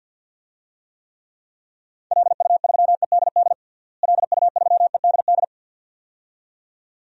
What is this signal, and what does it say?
Morse code, heard as a keyed tone, about 47 wpm, 700 Hz. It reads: LU3EDD LU3EDD
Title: DXCC-100 Call Signs - 6 Characters - Repeated 2x - 50wpm